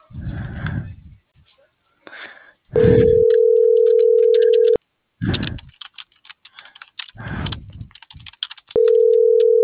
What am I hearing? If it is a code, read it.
Ringback tone (North America), 440 + 480 Hz.